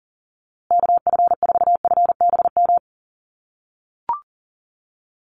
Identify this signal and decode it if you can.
Morse code, heard as a keyed tone, about 40 wpm, 700 Hz. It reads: KF4FBM
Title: USA Call Signs Repeated 2x 40wpm